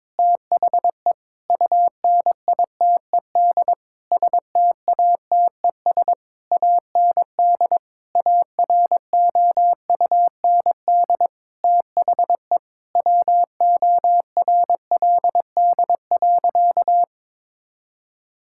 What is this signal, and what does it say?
Morse code, heard as a keyed tone, about 22 wpm, 700 Hz. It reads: THE UNITED STATES AND AROUND THE WORLD.
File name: practice / 22wpm / Sentences with 8-Word Phrases - 22wpm